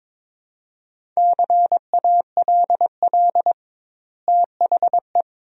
Morse code, heard as a keyed tone, about 22 wpm, 700 Hz. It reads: CALL THE
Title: 2x Word-Spacing - Sentences from Top 300 Words - 22wpm